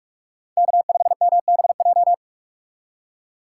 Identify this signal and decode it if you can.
Morse code, heard as a keyed tone, about 41 wpm, 700 Hz. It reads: K5MBJ
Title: USA Call Signs - 5 Characters - 4x Character Spacing - 45wpm